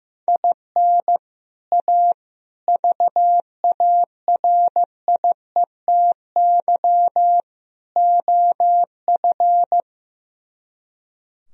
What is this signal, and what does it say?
Morse code, heard as a keyed tone, about 15 wpm, 700 Hz. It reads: IN A VARIETY OF
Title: Speed-Racing - Sets of 4 Words 15wpm